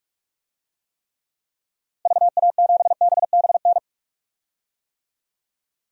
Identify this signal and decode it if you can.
Morse code, heard as a keyed tone, about 45 wpm, 700 Hz. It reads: VA7BBN
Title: DXCC-100 Call Signs - 6 Characters - Repeated 3x - 45wpm